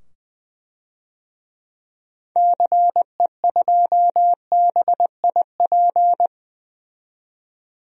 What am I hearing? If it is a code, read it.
Morse code, heard as a keyed tone, about 20 wpm, 700 Hz. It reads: CE2BIP
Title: DXCC-100 Call Signs - 6 Characters - 20wpm